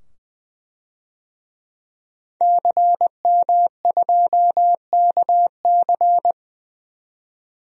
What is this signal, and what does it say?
Morse code, heard as a keyed tone, about 20 wpm, 700 Hz. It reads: CM2KC
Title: DXCC-100 Call Signs - 5 Characters - 20wpm